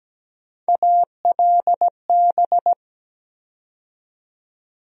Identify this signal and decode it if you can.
Morse code, heard as a keyed tone, about 17 wpm, 700 Hz. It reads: ALB
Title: USA Call Sign Postfixes - 3 Letters 17wpm